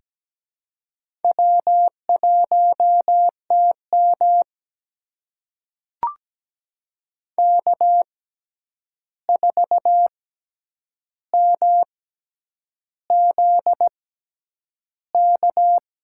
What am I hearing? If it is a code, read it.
Morse code, heard as a keyed tone, about 17 wpm, spaced as for 8 wpm, 700 Hz. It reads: W1TM K4MZK
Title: USA Call Signs - 6x Character Spacing - 17wpm